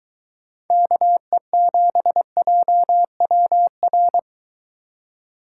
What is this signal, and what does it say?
Morse code, heard as a keyed tone, about 22 wpm, 700 Hz. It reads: KE7JWR